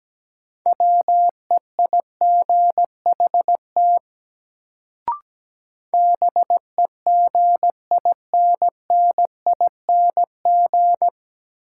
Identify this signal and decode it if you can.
Morse code, heard as a keyed tone, about 17 wpm, 700 Hz. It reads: WEIGHT BEGINNING
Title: Sentences from Top 800 Words - New Word Drill - 17wpm